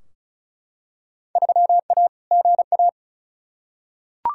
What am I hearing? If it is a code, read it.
Morse code, heard as a keyed tone, about 36 wpm, 700 Hz. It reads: 3A GA
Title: Field Day Exchanges - 35wpm